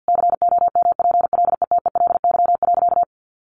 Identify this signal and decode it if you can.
Morse code, heard as a keyed tone, about 48 wpm, 700 Hz. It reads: COMPLETELY.
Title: Sentences from Top 1400 Words 50wpm